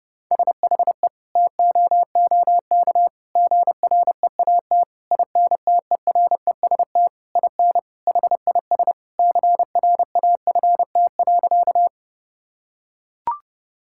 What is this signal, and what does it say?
Morse code, heard as a keyed tone, about 30 wpm, 700 Hz. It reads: SHE TOOK GREAT INTEREST IN HIS CRAFT.
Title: Sentences from Top 1000 Words - Review Part 6 - 30wpm